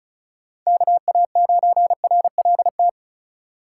Morse code, heard as a keyed tone, about 34 wpm, 700 Hz. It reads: KA9RLT
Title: USA Call Signs - 3x Character Spacing - 35wpm